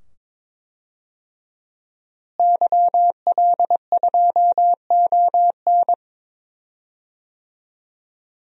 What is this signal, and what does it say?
Morse code, heard as a keyed tone, about 22 wpm, 700 Hz. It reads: YL2ON
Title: DXCC-100 Call Signs - 5 Characters - Repeated 2x - 22wpm